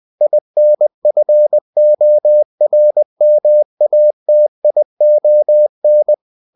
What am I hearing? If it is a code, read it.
Morse code, heard as a keyed tone, about 20 wpm, 600 Hz. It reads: INFORMATION